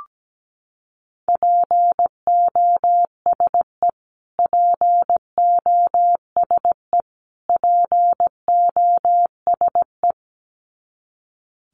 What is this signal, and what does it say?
Morse code, heard as a keyed tone, about 17 wpm, 700 Hz. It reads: POSE POSE POSE
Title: Top 500 Words Repeated 17wpm